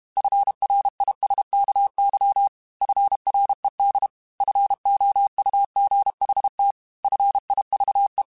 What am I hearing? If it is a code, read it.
Morse code, heard as a keyed tone, about 32 wpm, 800 Hz. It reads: FRISKY FRED FOUGHT FIVE
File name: F Runs / F Runs mp3s / F Runs @30